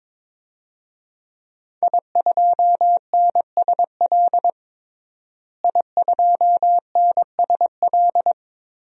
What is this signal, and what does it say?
Morse code, heard as a keyed tone, about 21 wpm, 700 Hz. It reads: I2NSL I2NSL
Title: DXCC-100 Call Signs - 5 Characters - Repeated 2x - 22wpm